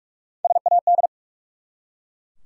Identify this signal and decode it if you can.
Morse code, heard as a keyed tone, about 44 wpm, 700 Hz. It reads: SAD
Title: Three Letter Words - 45wpm